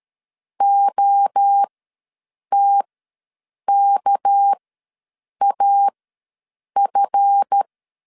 Morse code, heard as a keyed tone, about 13 wpm, spaced as for 7 wpm, 800 Hz. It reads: OTKAF
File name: LEZIONI / 8 YJ45 / 8 YJ4 ws 12wpm 3249